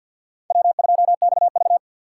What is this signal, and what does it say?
Morse code, heard as a keyed tone, about 49 wpm, 700 Hz. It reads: W2XV
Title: USA Call Signs - 2x Character Spacing - 50wpm